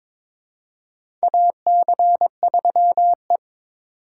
Morse code, heard as a keyed tone, about 21 wpm, 700 Hz. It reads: AC3E
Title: USA Call Signs - 4 Characters - 3x Character Spacing - 22wpm